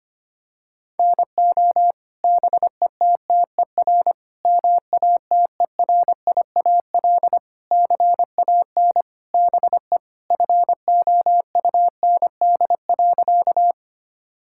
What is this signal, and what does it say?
Morse code, heard as a keyed tone, about 25 wpm, 700 Hz. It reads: NO BETTER MATERIAL CAN BE FOUND.
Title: Sentences from Top 1000 Words - Review Part 5 - 25wpm